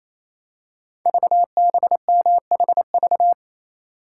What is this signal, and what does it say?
Morse code, heard as a keyed tone, about 29 wpm, 700 Hz. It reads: VBMHV
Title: Code Groups - 5 Letters - 28wpm